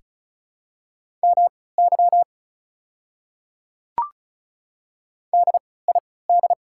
Morse code, heard as a keyed tone, about 36 wpm, spaced as for 20 wpm, 690 Hz. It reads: MY DID
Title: 3x Character Spacing - Top 100 Words - 35wpm